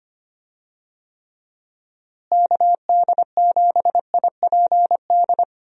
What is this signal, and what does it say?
Morse code, heard as a keyed tone, about 25 wpm, 700 Hz. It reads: KD7IPD
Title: USA Call Signs - 6 Characters - Repeated 2x - 25wpm